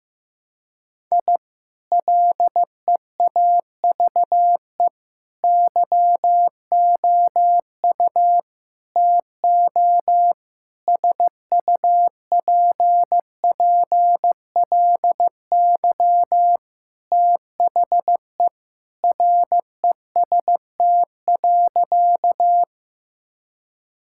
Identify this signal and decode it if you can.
Morse code, heard as a keyed tone, about 15 wpm, 700 Hz. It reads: I LEAVE YOU TO SUPPLY THE REST.
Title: Sentences from Top 1000 Words - Review Part 5 - 15wpm